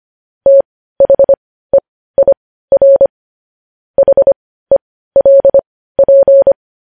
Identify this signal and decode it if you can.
Morse code, heard as a keyed tone, about 25 wpm, spaced as for 15 wpm, 550 Hz. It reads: THEIR HELP
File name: podcast / News headlines-1x-25x15